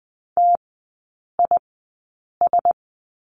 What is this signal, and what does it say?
Morse code, heard as a keyed tone, about 20 wpm, spaced as for 8 wpm, 700 Hz. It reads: TIS
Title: 100 Lesson 17e - Letter u - Three Characters 20wpm